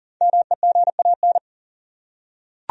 Morse code, heard as a keyed tone, about 38 wpm, 700 Hz. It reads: MEGAN
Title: USA Common Names - 40wpm